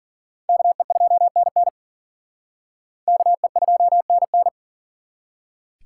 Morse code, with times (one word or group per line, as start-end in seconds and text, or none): 0.5-1.7 KE2NN
3.1-4.5 KE2NN